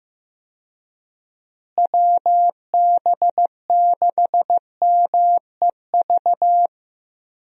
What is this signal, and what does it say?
Morse code, heard as a keyed tone, about 16 wpm, 700 Hz. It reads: WB6MEV
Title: USA Call Signs - 6 Characters - Repeated 3x - 15wpm